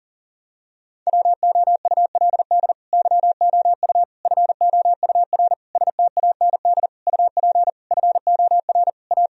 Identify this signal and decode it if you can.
Morse code, heard as a keyed tone, about 40 wpm, 700 Hz. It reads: WOULD YOU FOUR STAND UP FOR A
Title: Sentences from Top 1000 Words - Review Part 1 - 40wpm